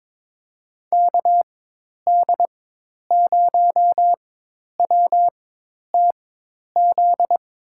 Morse code, heard as a keyed tone, about 21 wpm, spaced as for 10 wpm, 700 Hz. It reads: KD0WTZ
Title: USA Call Signs - 6 Characters - 4x Character Spacing - 22wpm